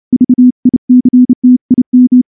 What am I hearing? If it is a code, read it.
Morse code, heard as a keyed tone, about 30 wpm, 250 Hz. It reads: VICTIM
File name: morse-training-30WPM